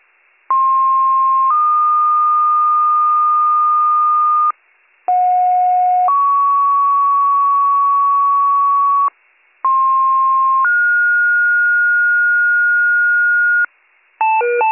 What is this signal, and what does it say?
Two-tone page, heard as a one-second tone, then a three-second tone, about 1060 then 1220 Hz, about 730 then 1120 Hz, about 1060 then 1430 Hz, with noise. It is heard from 0.5 to 13.7 s.